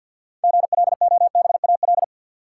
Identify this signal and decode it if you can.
Morse code, heard as a keyed tone, about 53 wpm, 700 Hz. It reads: GLOBAL
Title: Sentences from Top 1200 Words - New Word Drill - 50wpm